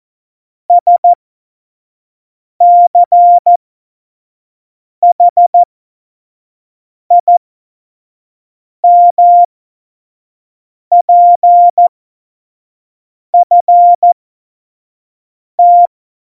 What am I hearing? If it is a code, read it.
Morse code, heard as a keyed tone, about 14 wpm, spaced as for 5 wpm, 700 Hz. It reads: SCHIMPFT
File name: Morsekurs / Witze / Witz 15. Mai 2018, 25 F70